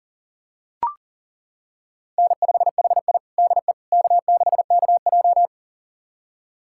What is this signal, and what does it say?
Morse code, heard as a keyed tone, about 40 wpm, 700 Hz. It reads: N5HI DE K6KJ
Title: USA Callsign DE Callsign - 4 Characters - 40wpm